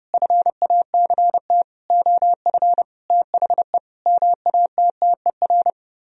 Morse code, heard as a keyed tone, about 40 wpm, 700 Hz. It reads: FACT OF THE MATTER